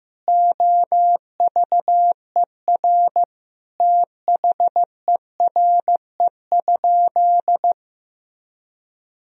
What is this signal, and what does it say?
Morse code, heard as a keyed tone, about 15 wpm, 700 Hz. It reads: OVER THERE?
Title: Sentences from Top 200 Words and Abbreviations 15wpm